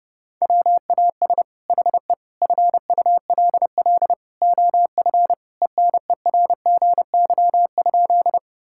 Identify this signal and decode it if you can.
Morse code, heard as a keyed tone, about 30 wpm, 700 Hz. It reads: WAS HE FULL OF ENERGY?
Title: Sentences from Top 1000 Words - Review Part 6 - 30wpm